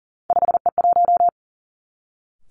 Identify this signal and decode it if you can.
Morse code, heard as a keyed tone, about 38 wpm, 700 Hz. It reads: 5E1
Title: Three Challenging Letter-Number Combinations 40wpm